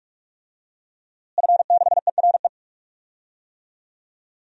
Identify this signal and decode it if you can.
Morse code, heard as a keyed tone, about 43 wpm, 700 Hz. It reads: F6ERE
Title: DXCC-100 Call Signs - 5 Characters - Repeated 3x - 45wpm